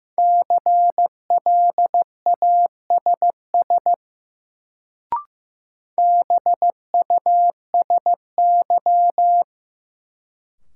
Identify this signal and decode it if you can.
Morse code, heard as a keyed tone, about 15 wpm, 700 Hz. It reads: CLASS BUSY
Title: Sentences from Top 1000 Words - Review Part 2 - New Word Drill - 15wpm